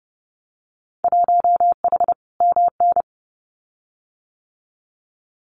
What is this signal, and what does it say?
Morse code, heard as a keyed tone, about 30 wpm, 700 Hz. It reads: 1H MN
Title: Winter Field Day Exchanges 30wpm